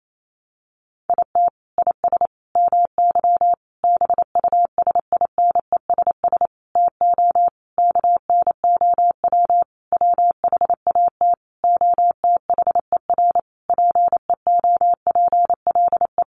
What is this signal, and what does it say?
Morse code, heard as a keyed tone, about 28 wpm, 700 Hz. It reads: IT IS MY BUSINESS TO KNOW WHAT OTHER PEOPLE
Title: Sentences from Top 600 Words 28wpm